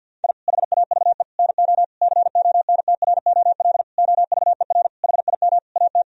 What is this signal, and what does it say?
Morse code, heard as a keyed tone, about 50 wpm, 700 Hz. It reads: I HAVE NO CONTROL OVER HIM AT
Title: Sentences from Top 1000 Words - Review Part 3 - 50wpm